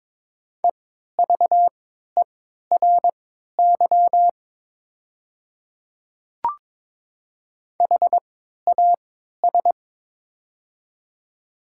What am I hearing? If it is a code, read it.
Morse code, heard as a keyed tone, about 22 wpm, spaced as for 12 wpm, 700 Hz. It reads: EVERY HAS
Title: 3x Character Spacing - Top 200 Words - 22wpm